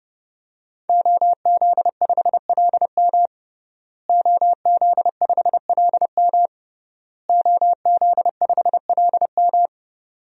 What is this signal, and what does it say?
Morse code, heard as a keyed tone, about 30 wpm, 700 Hz. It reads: OZ5LM OZ5LM OZ5LM
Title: DXCC-100 Call Signs - 5 Characters - Repeated 3x - 30wpm